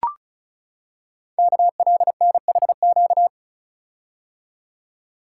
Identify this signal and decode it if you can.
Morse code, heard as a keyed tone, about 36 wpm, 690 Hz. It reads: KLNHQ
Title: Code Groups - 5 Letters - 35wpm